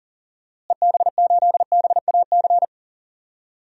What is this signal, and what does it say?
Morse code, heard as a keyed tone, about 38 wpm, 700 Hz. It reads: EB8BAC
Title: DXCC-100 Call Signs - 6 Characters - Repeated 2x - 40wpm